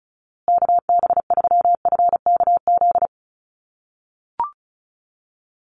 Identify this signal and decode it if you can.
Morse code, heard as a keyed tone, about 34 wpm, 700 Hz. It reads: KB3FKZ
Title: USA Call Signs Repeated 3x 35wpm